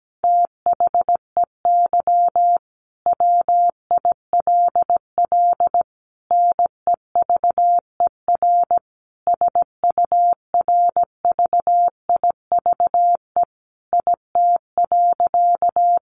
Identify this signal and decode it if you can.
Morse code, heard as a keyed tone, about 17 wpm, 700 Hz. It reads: THEY WILL NEVER SURVIVE IT.
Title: Sentences from Top 1300 Words 17wpm